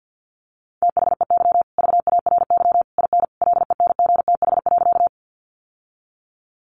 Morse code, heard as a keyed tone, about 50 wpm, 700 Hz. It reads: THEY VARY IN LENGTH.
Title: Sentences from Top 1200 Words 50wpm